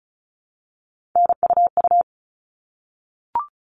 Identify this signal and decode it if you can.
Morse code, heard as a keyed tone, about 34 wpm, 700 Hz. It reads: NUU
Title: All Cut Numbers - 3-Digit 35wpm